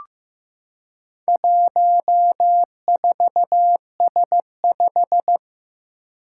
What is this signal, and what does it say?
Morse code, heard as a keyed tone, about 15 wpm, 700 Hz. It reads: 14S5
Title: Four Challenging Letter-Number Combinations 15wpm